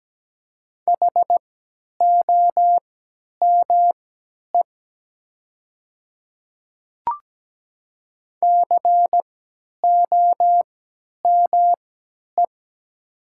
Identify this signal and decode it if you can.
Morse code, heard as a keyed tone, about 17 wpm, spaced as for 10 wpm, 700 Hz. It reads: HOME COME
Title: 3x Character Spacing - Top 200 Words - 17wpm